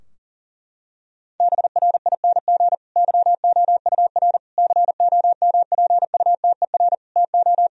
Morse code, heard as a keyed tone, about 40 wpm, 700 Hz. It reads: BRING YOUR COMPUTER TO
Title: Sentences from Top 1000 Words - Review Part 3 - 40wpm